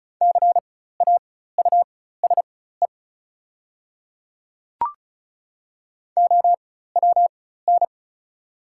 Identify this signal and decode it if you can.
Morse code, heard as a keyed tone, about 34 wpm, spaced as for 16 wpm, 690 Hz. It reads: CAUSE OWN
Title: 4x Character Spacing - Top 300 Words - 35wpm